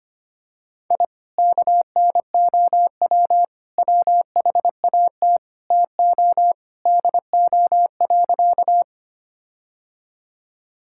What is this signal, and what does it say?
Morse code, heard as a keyed tone, about 25 wpm, 700 Hz. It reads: I KNOW WHAT TO DO.
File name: practice / 25wpm / Sentences with 4-Word Phrases - 25wpm